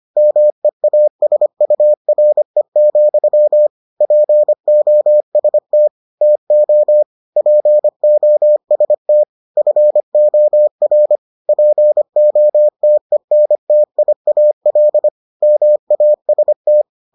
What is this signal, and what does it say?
Morse code, heard as a keyed tone, about 25 wpm, 600 Hz. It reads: MEASURE, POST TO POST FOR POTENTIAL MAST